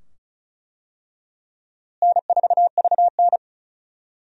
Morse code, heard as a keyed tone, about 34 wpm, 700 Hz. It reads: N4VN
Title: USA Call Signs - 4 Characters - Repeated 2x - 35wpm